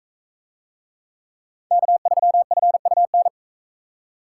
Morse code, heard as a keyed tone, about 41 wpm, 700 Hz. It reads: K3FUN